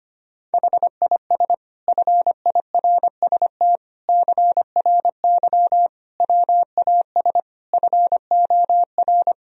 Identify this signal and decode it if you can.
Morse code, heard as a keyed tone, about 25 wpm, 700 Hz. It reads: HIS FIRST CRY WAS FOR
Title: Sentences from Top 1000 Words - Review Part 2 - 25wpm